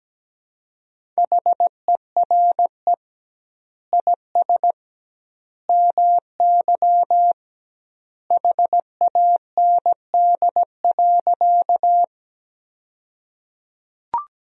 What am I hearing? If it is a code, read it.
Morse code, heard as a keyed tone, about 17 wpm, 700 Hz. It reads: HERE IS MY HAND.
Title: 2x Word-Spacing - Sentences from Top 200 Words - 17wpm